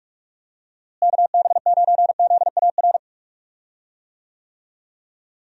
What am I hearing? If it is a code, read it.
Morse code, heard as a keyed tone, about 46 wpm, 700 Hz. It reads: KB9ZAR
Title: USA Call Signs - 6 Characters - Repeated 3x - 45wpm